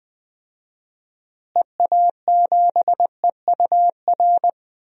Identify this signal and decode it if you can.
Morse code, heard as a keyed tone, about 20 wpm, 700 Hz. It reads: EA7EUR
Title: DXCC-100 Call Signs - 6 Characters - Repeated 2x - 20wpm